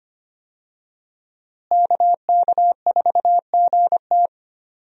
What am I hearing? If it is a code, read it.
Morse code, heard as a keyed tone, about 25 wpm, 700 Hz. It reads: KK4GT
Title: USA Call Signs - 5 Characters - Repeated 2x - 25wpm